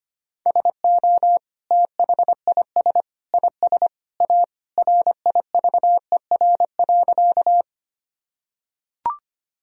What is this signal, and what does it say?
Morse code, heard as a keyed tone, about 25 wpm, 700 Hz. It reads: SO THIS IS A RIVER.
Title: Sentences from Top 1000 Words - Review Part 1 - 25wpm